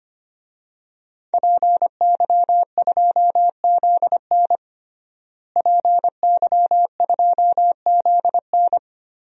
Morse code, heard as a keyed tone, about 25 wpm, 700 Hz. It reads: PY2ZN PY2ZN
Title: DXCC-100 Call Signs - 5 Characters - Repeated 3x - 25wpm